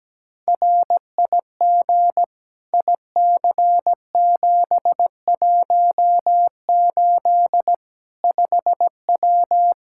Morse code, heard as a keyed tone, about 17 wpm, 700 Hz. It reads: RIG IC718 5W
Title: K1USN SST QSO Practice 17wpm